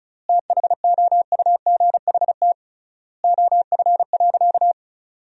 Morse code, heard as a keyed tone, about 36 wpm, 700 Hz. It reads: THOUGHT OF.
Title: 3x Word-Spacing - Sentences from Top 300 Words - 35wpm